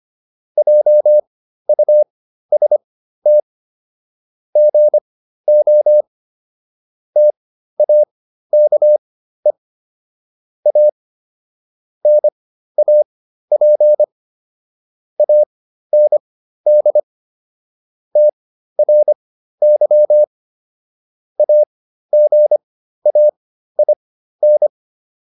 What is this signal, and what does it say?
Morse code, heard as a keyed tone, about 25 wpm, spaced as for 13 wpm, 600 Hz. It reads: JUST GO TAKE A NAP AND TRY AGAIN